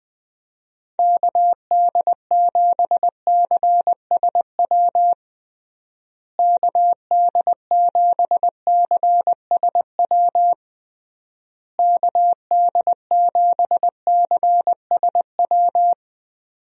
Morse code, heard as a keyed tone, about 20 wpm, 700 Hz. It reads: KD7CSW KD7CSW KD7CSW
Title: USA Call Signs - 6 Characters - Repeated 3x - 20wpm